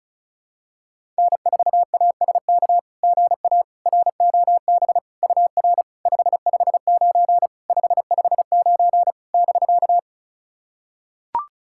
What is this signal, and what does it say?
Morse code, heard as a keyed tone, about 36 wpm, 690 Hz. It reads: N4ASK GA ROB UR 559 559 <BK>
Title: SOTA QSO Practice 35wpm